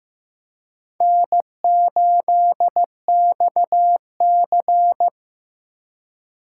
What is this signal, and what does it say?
Morse code, heard as a keyed tone, about 15 wpm, 700 Hz. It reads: N8XC
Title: USA Call Signs - 4x Character Spacing - 15wpm